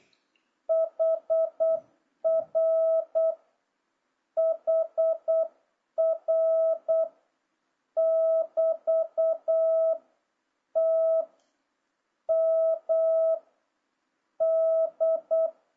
Morse code, heard as a keyed tone, about 8 wpm, 650 Hz. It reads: HR HR <BT> T M D
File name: jh8cbh / 2026cw